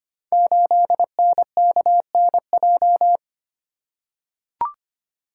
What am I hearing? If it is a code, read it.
Morse code, heard as a keyed tone, about 25 wpm, 700 Hz. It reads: 8NKNJ
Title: Code Groups - 5 Letter:Numbers - 25wpm